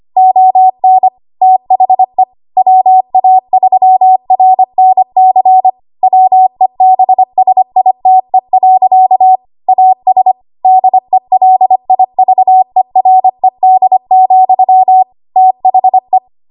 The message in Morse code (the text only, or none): ON THE WA3RNC WEBSITE. AS DELIVERED, THE